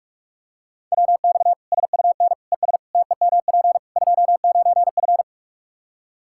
Morse code, heard as a keyed tone, about 45 wpm, 700 Hz. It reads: WX SUN ES TEMP 29F
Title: Practice QSO - Exchange 2 - Info - 45wpm